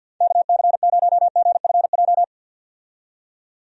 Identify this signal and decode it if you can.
Morse code, heard as a keyed tone, about 49 wpm, 700 Hz. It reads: KC0GFJ
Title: USA Call Signs - 6 Characters - 50wpm